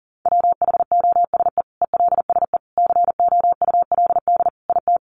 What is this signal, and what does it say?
Morse code, heard as a keyed tone, about 40 wpm, 700 Hz. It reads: WHOSE ELSE COULD IT
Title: Sentences from Top 1200 Words 40wpm